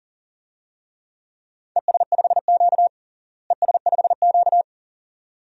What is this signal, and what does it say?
Morse code, heard as a keyed tone, about 37 wpm, 700 Hz. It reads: ES5Q ES5Q
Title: DXCC-100 Call Signs - 4 Characters - Repeated 2x - 40wpm